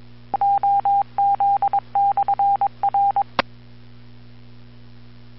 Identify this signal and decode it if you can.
Morse code, heard as a keyed tone, about 22 wpm, 800 Hz, with radio noise. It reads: OZ/R
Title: Bluemont, VA Skywarn Repeater - WA4TSC